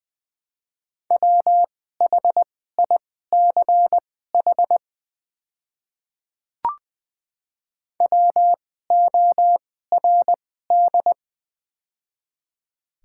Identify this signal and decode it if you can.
Morse code, heard as a keyed tone, about 20 wpm, spaced as for 14 wpm, 700 Hz. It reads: WHICH WORD
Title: 2x Character Spacing - Top 100 Words - 20wpm